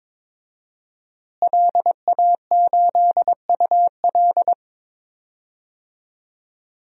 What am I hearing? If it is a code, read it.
Morse code, heard as a keyed tone, about 21 wpm, 700 Hz. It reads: LA8UL
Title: DXCC-100 Call Signs - 5 Characters - Repeated 2x - 22wpm